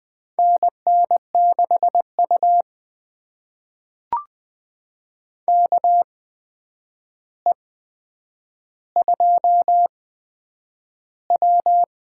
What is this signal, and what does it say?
Morse code, heard as a keyed tone, about 20 wpm, spaced as for 9 wpm, 700 Hz. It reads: NN6U KE2W
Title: USA Call Signs - 4 Characters - 8x Character Spacing - 20wpm